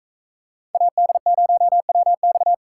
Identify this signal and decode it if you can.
Morse code, heard as a keyed tone, about 45 wpm, 700 Hz. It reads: AD0WX